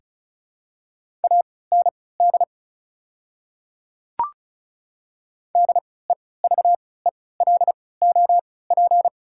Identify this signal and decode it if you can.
Morse code, heard as a keyed tone, about 36 wpm, spaced as for 20 wpm, 700 Hz. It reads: AND DEVELOP
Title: 3x Character Spacing - Top 500 Words - 35wpm